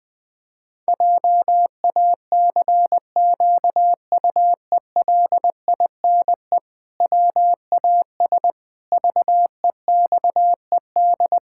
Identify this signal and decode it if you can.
Morse code, heard as a keyed tone, about 20 wpm, 700 Hz. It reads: JACQUELINE WAS VEXED
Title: Pangram Challenge v2 - 20wpm